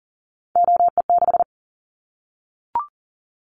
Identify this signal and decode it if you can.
Morse code, heard as a keyed tone, about 42 wpm, 700 Hz. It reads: OE6
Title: Callsign Prefixes for Top 50 DXCC - 40wpm